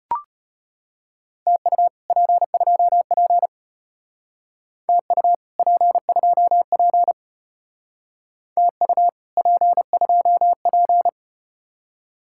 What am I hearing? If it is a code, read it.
Morse code, heard as a keyed tone, about 32 wpm, 700 Hz. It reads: TU P2P TU P2P TU P2P